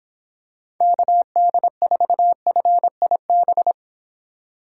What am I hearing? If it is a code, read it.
Morse code, heard as a keyed tone, about 26 wpm, 700 Hz. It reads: KD4FIB
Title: Speed Racing - USA 2x3 Call Signs 22wpm